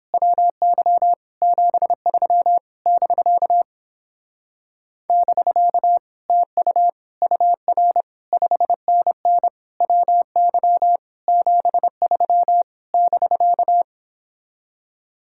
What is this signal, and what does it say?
Morse code, heard as a keyed tone, about 29 wpm, 700 Hz. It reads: WY 73 <BK> <BK> TU UR 5NN WY 73 <BK>